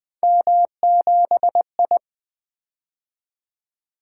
Morse code, heard as a keyed tone, about 20 wpm, 700 Hz. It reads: M7I
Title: Three Letter-Number Combinations - 20wpm